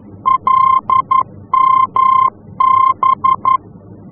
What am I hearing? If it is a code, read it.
Morse code, heard as a keyed tone, about 11 wpm, 1050 Hz. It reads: LMB
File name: france / mx / LMB-F-VOR